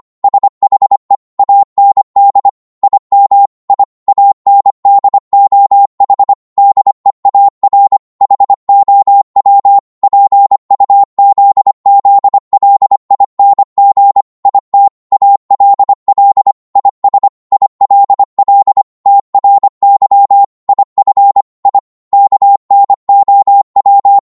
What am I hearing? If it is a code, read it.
Morse code, heard as a keyed tone, about 25 wpm, 800 Hz. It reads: SHE AND IM I AND OH DEAR HOW PUZZLING IT ALL IS ILL TRY IF I KNOW